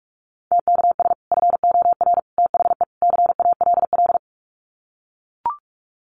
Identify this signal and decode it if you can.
Morse code, heard as a keyed tone, about 44 wpm, 700 Hz. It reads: TKS FOR THE CALL
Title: Common QSO Phrases 45wpm